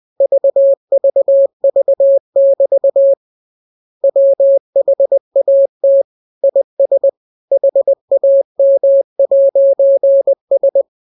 Morse code, heard as a keyed tone, about 20 wpm, 550 Hz. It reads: VVV<BT> WHAT IS HAM'S
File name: jokes / joke 6 20 0000